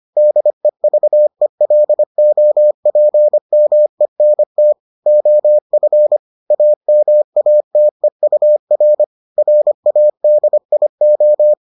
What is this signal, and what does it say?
Morse code, heard as a keyed tone, about 25 wpm, 600 Hz. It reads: DEVELOPMENT OF AMATEUR RADIO